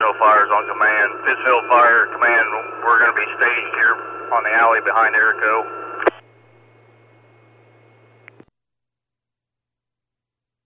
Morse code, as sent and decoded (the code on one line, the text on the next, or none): none